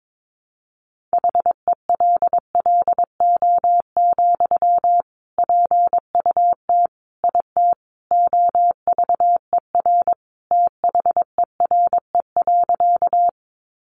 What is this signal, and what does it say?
Morse code, heard as a keyed tone, about 22 wpm, 700 Hz. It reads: HELLO, PUT IT OVER THERE.
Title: Sentences from Top 1100 Words 22wpm